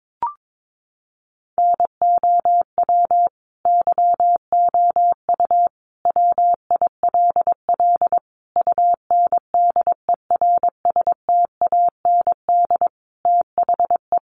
Morse code, heard as a keyed tone, about 22 wpm, 700 Hz. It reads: NOW YOU WILL UNDERSTAND THE
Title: Sentences from Top 1200 Words 22wpm